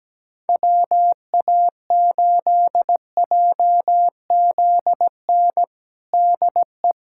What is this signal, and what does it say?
Morse code, heard as a keyed tone, about 17 wpm, 700 Hz. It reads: WA8JZN DE
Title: USA Callsign DE Callsign - 6 Characters - 17wpm